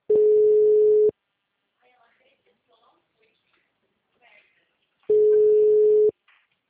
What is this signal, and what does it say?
Ringback tone (Europe, 425 Hz), with speech.